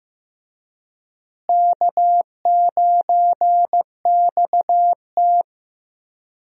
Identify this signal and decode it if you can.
Morse code, heard as a keyed tone, about 16 wpm, 700 Hz. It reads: K9XT
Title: USA Call Signs - 4 Characters - Repeated 2x - 15wpm